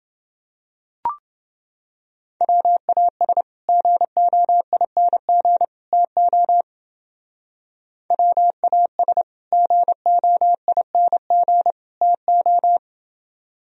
Morse code, heard as a keyed tone, about 30 wpm, 700 Hz. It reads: WAS GOING TO WAS GOING TO